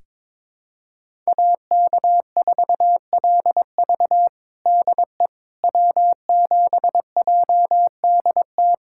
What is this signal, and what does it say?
Morse code, heard as a keyed tone, about 22 wpm, 700 Hz. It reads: AK4LV DE W7JDT
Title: USA Callsign DE Callsign - 5 Characters - 22wpm